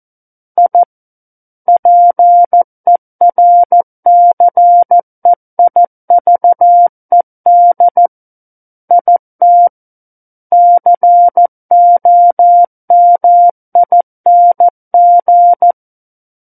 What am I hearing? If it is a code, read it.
Morse code, heard as a keyed tone, about 14 wpm, 700 Hz. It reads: I PERCEIVED IT COMING
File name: Book One / 05-Chapter 5 - 14 WPM